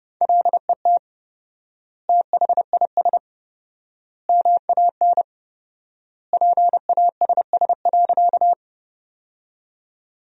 Morse code, heard as a keyed tone, about 28 wpm, 700 Hz. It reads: LET THIS MAN PASS.
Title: 4x Word-Spacing - Sentences from Top 400 Words - 30wpm